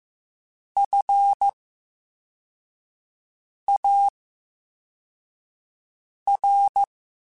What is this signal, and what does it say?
Morse code, heard as a keyed tone, about 15 wpm, spaced as for 4 wpm, 800 Hz. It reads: FAR
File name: Special INGs-mp3s / Special INGs-mp3s / VerySlowToFaster Ing @13